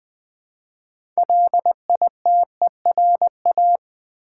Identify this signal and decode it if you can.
Morse code, heard as a keyed tone, about 20 wpm, 700 Hz. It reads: LITERA
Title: Build A Word - 3 Letters At A Time - 9-Letter Words - 20wpm